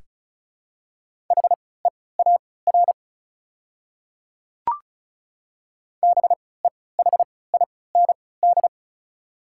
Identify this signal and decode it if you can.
Morse code, heard as a keyed tone, about 36 wpm, spaced as for 20 wpm, 690 Hz. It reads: HEAR BEHIND
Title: 3x Character Spacing - Top 500 Words - 35wpm